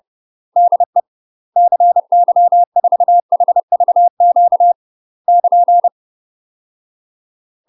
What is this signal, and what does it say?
Morse code, heard as a keyed tone, about 30 wpm, 700 Hz. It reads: DE CY4HVQ <KN>